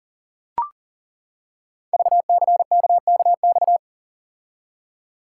Morse code, heard as a keyed tone, about 41 wpm, 700 Hz. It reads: VCKKX
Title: Code Groups - 5 Letters - 40wpm